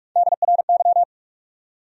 Morse code, heard as a keyed tone, about 59 wpm, 700 Hz. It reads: DRY